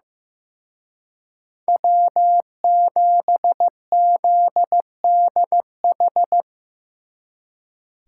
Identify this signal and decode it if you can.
Morse code, heard as a keyed tone, about 17 wpm, 700 Hz. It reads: W7ZDH
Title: Speed Racing - USA Call Signs 15wpm